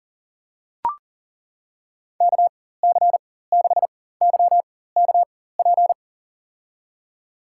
Morse code, heard as a keyed tone, about 38 wpm, spaced as for 18 wpm, 700 Hz. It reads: KC6YKP
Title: USA Call Signs - 6 Characters - 4x Character Spacing - 40wpm